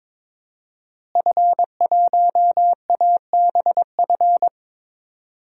Morse code, heard as a keyed tone, about 22 wpm, 700 Hz. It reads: F1ABF
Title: DXCC-100 Call Signs - 5 Characters - Repeated 3x - 22wpm